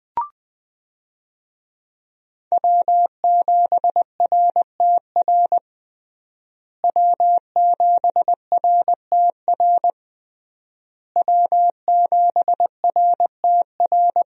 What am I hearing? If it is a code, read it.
Morse code, heard as a keyed tone, about 21 wpm, 700 Hz. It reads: W7RTR W7RTR W7RTR
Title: USA Call Signs - 5 Characters - Repeated 3x - 20wpm